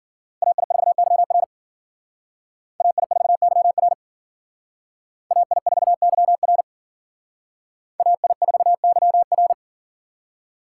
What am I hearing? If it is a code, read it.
Morse code, heard as a keyed tone, about 51 wpm, 700 Hz. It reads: AI4YR AI4YR AI4YR AI4YR